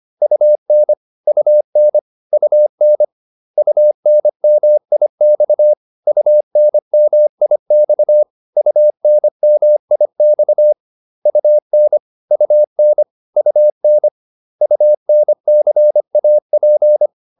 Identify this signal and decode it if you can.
Morse code, heard as a keyed tone, about 25 wpm, 600 Hz. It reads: UN UN UN UNMIX UNMIX UNMIX UN UN UN UNCAP